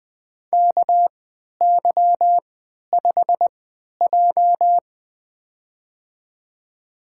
Morse code, heard as a keyed tone, about 20 wpm, spaced as for 11 wpm, 700 Hz. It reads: KY5J
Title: USA Call Signs - 3x Character Spacing - 20wpm